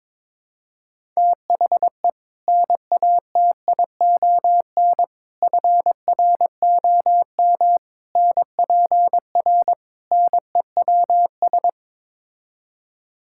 Morse code, heard as a keyed tone, about 22 wpm, 700 Hz. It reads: THE NATION FROM NPR NEWS
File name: practice / 22wpm / Build A Phrase - 5-Word Phrases - 22wpm